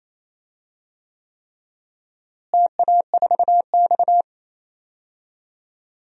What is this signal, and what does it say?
Morse code, heard as a keyed tone, about 28 wpm, 700 Hz. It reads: TA4X
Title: DXCC-100 Call Signs - 4 Characters - Repeated 3x - 28wpm